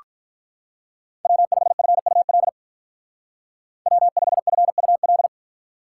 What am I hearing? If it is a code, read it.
Morse code, heard as a keyed tone, about 45 wpm, 700 Hz. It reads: W5FUL W5FUL